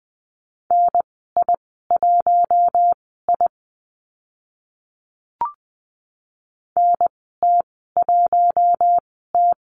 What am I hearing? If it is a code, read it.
Morse code, heard as a keyed tone, about 20 wpm, spaced as for 15 wpm, 700 Hz. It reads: NI1I NT1T
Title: 042 Lesson 8g - Number 1 - Call Signs 20wpm